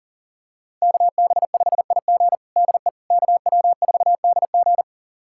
Morse code, heard as a keyed tone, about 40 wpm, 700 Hz. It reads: KB5IG DE KW4DG
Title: USA Callsign DE Callsign - 5 Characters - 40wpm